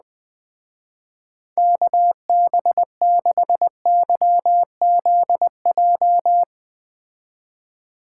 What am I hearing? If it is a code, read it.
Morse code, heard as a keyed tone, about 21 wpm, 700 Hz. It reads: KB6YZJ
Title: Speed Racing - USA 2x3 Call Signs 17wpm